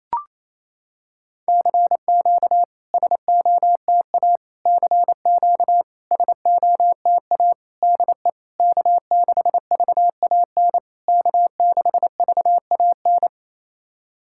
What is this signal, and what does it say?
Morse code, heard as a keyed tone, about 28 wpm, 700 Hz. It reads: CQ SOTA CQ SOTA DE K6VAN K6VAN
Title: SOTA QSO Practice 28wpm